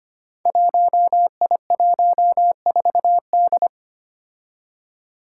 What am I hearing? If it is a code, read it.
Morse code, heard as a keyed tone, about 25 wpm, 700 Hz. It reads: 1I14D
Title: Five Challenging Letter-Number Combinations 25wpm